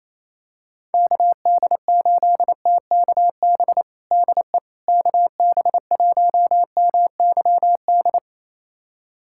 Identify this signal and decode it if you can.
Morse code, heard as a keyed tone, about 28 wpm, 700 Hz. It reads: KD8TKB DE KB1MYD
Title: USA Callsign DE Callsign - 6 Characters - 28wpm